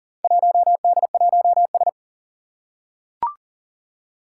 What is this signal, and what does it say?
Morse code, heard as a keyed tone, about 40 wpm, 700 Hz. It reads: 1D1S
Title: Four Challenging Letter-Number Combinations 40wpm